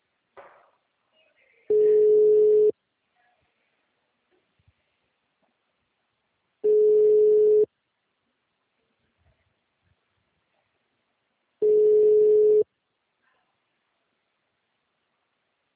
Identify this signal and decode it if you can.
Ringback tone (Europe, 425 Hz).
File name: out- -2004- - - .1336